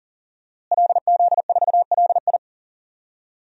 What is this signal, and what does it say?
Morse code, heard as a keyed tone, about 37 wpm, 700 Hz. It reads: LZ4LI
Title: DXCC-100 Call Signs - 5 Characters - Repeated 3x - 40wpm